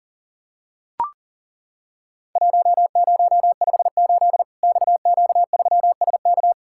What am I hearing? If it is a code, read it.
Morse code, heard as a keyed tone, about 40 wpm, 700 Hz. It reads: 1058 XQ3SK
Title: Worked All Europe DX Contest - QTC Exchange - 40wpm